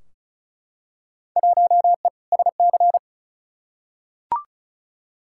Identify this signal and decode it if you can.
Morse code, heard as a keyed tone, about 36 wpm, 700 Hz. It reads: 1E SC
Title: Field Day Exchanges - 35wpm